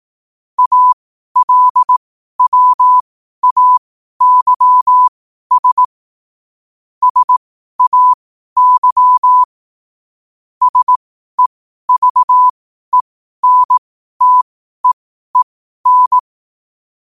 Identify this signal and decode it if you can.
Morse code, heard as a keyed tone, about 18 wpm, spaced as for 12 wpm, 1000 Hz. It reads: ALWAYS SAY SEVENTEEN
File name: qotdfiles / 12wpm